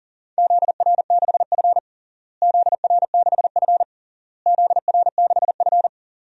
Morse code, heard as a keyed tone, about 37 wpm, 700 Hz. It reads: ZR6F ZR6F ZR6F
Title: DXCC-100 Call Signs - 4 Characters - Repeated 3x - 40wpm